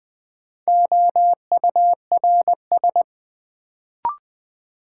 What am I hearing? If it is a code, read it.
Morse code, heard as a keyed tone, about 20 wpm, 700 Hz. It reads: OURS
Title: Sentences from Top 1300 Words - New Word Drill Alt - 20wpm